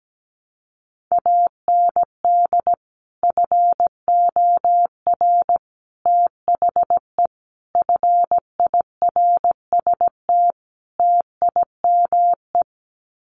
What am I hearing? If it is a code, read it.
Morse code, heard as a keyed tone, about 17 wpm, 700 Hz. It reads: AND FOR THE FIRST TIME
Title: Sets of 5 Words 17wpm